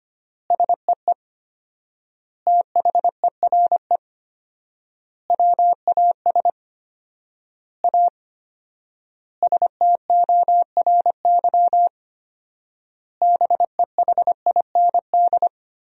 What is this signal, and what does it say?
Morse code, heard as a keyed tone, about 25 wpm, 700 Hz. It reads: SEE THERE WAS A STORY BEHIND
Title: 4x Word-Spacing - Sentences from Top 500 Words - 25wpm